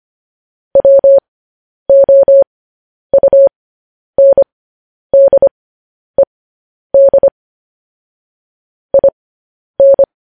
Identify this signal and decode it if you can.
Morse code, heard as a keyed tone, about 25 wpm, spaced as for 10 wpm, 550 Hz. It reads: WOUNDED IN
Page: news-headlines-in-morse-code-at-10-wpm